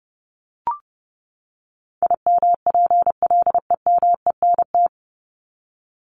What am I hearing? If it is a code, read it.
Morse code, heard as a keyed tone, about 30 wpm, 700 Hz. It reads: IMPLEMENT
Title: Top 1501-1750 Words 30wpm